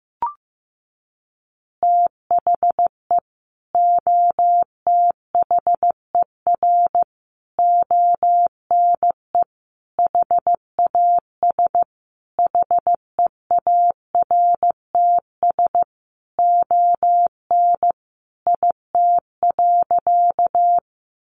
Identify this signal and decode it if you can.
Morse code, heard as a keyed tone, about 15 wpm, 700 Hz. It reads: THE OTHER ONE HAS HEARTS ON IT.
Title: Additional Easy Phrases 15wpm